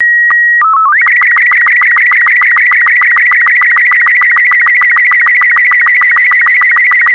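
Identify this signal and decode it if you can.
SSTV (slow-scan television), Robot 36.
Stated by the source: SSTV transmission offered as a ringtone on MeloBoom.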